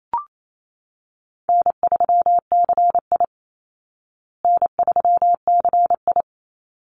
Morse code, heard as a keyed tone, about 28 wpm, 690 Hz. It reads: N3CI N3CI
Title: Call Signs Repeated 3x - 28wpm